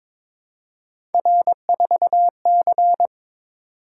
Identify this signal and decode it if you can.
Morse code, heard as a keyed tone, about 22 wpm, 700 Hz. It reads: R4C
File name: practice / 22wpm / Three Letter-Number Combinations - 22wpm